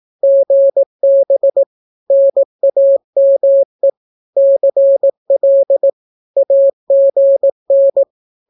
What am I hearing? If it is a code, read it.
Morse code, heard as a keyed tone, about 18 wpm, 550 Hz. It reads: GB NAME CL AGN
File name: naukapodstaw / skroty / skroty